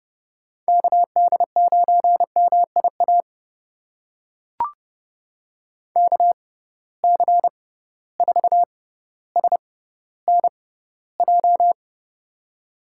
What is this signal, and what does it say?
Morse code, heard as a keyed tone, about 30 wpm, spaced as for 15 wpm, 700 Hz. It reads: KD9MIA KC4SNJ
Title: USA Call Signs - 6 Characters - 6x Character Spacing - 30wpm